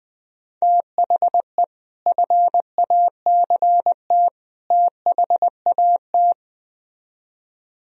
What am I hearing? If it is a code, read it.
Morse code, heard as a keyed tone, about 21 wpm, 700 Hz. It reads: THE FACT THAT